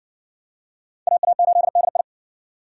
Morse code, heard as a keyed tone, about 55 wpm, 700 Hz. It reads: AA8DI